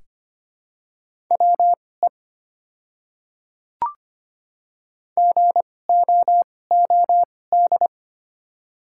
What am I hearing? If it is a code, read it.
Morse code, heard as a keyed tone, about 25 wpm, spaced as for 18 wpm, 690 Hz. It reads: WE GOOD
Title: 2x Character Spacing - Top 200 Words - 25wpm